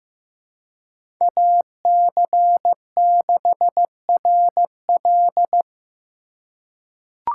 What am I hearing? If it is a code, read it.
Morse code, heard as a keyed tone, about 15 wpm, 700 Hz. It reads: AC6RL
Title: USA Call Signs - 5 Characters - 4x Character Spacing - 15wpm